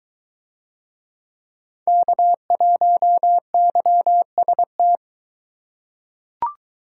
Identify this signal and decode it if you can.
Morse code, heard as a keyed tone, about 22 wpm, 700 Hz. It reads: K1YST